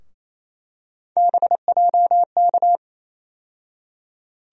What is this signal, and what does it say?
Morse code, heard as a keyed tone, about 28 wpm, 700 Hz. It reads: BJK
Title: USA Call Sign Postfixes - 3 Letters 28wpm